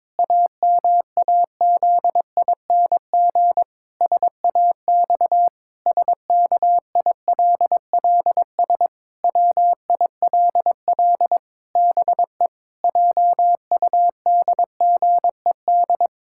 Morse code, heard as a keyed tone, about 22 wpm, 700 Hz. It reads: AMAZING SAX SKILLS WILL BE JUDGED
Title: Pangram Challenge v2 - 22wpm